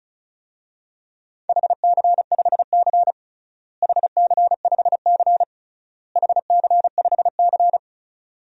Morse code, heard as a keyed tone, about 34 wpm, 700 Hz. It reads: HC5C HC5C HC5C
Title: DXCC-100 Call Signs - 4 Characters - Repeated 3x - 35wpm